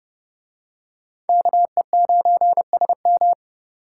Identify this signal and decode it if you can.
Morse code, heard as a keyed tone, about 28 wpm, 700 Hz. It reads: KE9SM